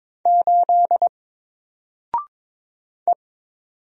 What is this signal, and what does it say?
Morse code, heard as a keyed tone, about 22 wpm, 700 Hz. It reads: 8 E
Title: Single Character (Letter:Number:Punctuation:Prosign) Alt 22wpm